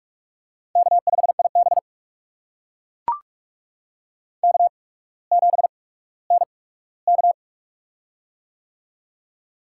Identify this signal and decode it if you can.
Morse code, heard as a keyed tone, about 42 wpm, spaced as for 19 wpm, 700 Hz. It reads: K5IB K7NK
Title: USA Call Signs - 4 Characters - 8x Character Spacing - 45wpm